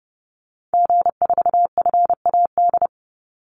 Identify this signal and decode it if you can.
Morse code, heard as a keyed tone, about 30 wpm, 700 Hz. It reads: G4FAD
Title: Call Signs - 30wpm